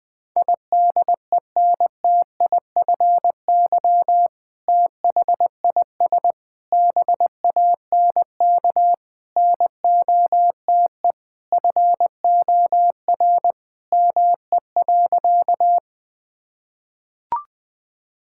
Morse code, heard as a keyed tone, about 20 wpm, 700 Hz. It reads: IDENTIFY THIS BANK NOTE FOR ME.
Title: Sentences from Top 1000 Words - Review Part 6 - 20wpm